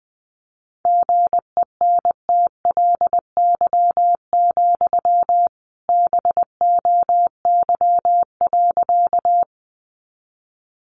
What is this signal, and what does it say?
Morse code, heard as a keyed tone, about 20 wpm, 700 Hz. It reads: GENTLY, BOY.
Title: Sentences from Top 1500 Words 20wpm